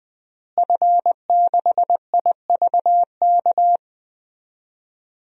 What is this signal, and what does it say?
Morse code, heard as a keyed tone, about 20 wpm, 700 Hz. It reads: F6IVK
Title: DXCC-100 Call Signs - 5 Characters - Repeated 3x - 20wpm